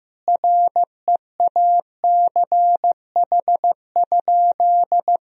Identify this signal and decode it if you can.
Morse code, heard as a keyed tone, about 15 wpm, 700 Hz. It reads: REACH?
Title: Sentences from Top 1000 Words - Review Part 4 - 15wpm